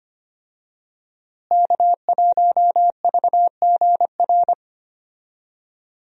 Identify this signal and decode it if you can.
Morse code, heard as a keyed tone, about 25 wpm, 700 Hz. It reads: K1VGR
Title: USA Call Signs - 5 Characters - Repeated 2x - 25wpm